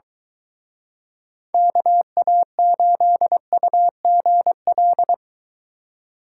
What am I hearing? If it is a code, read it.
Morse code, heard as a keyed tone, about 24 wpm, 700 Hz. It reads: KA8UGL